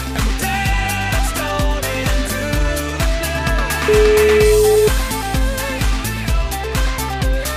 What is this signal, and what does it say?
Ringback tone (Europe, 425 Hz), with noise.